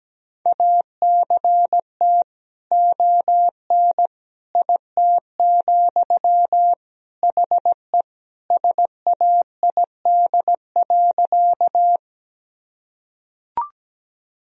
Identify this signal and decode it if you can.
Morse code, heard as a keyed tone, about 17 wpm, 700 Hz. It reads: ACT ON IT, HE SAID.
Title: Sentences from Top 1000 Words - Review Part 1 - 17wpm